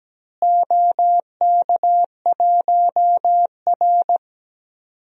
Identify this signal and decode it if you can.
Morse code, heard as a keyed tone, about 17 wpm, 700 Hz. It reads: OK1R
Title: DXCC-100 Call Signs - 4 Characters - Repeated 2x - 17wpm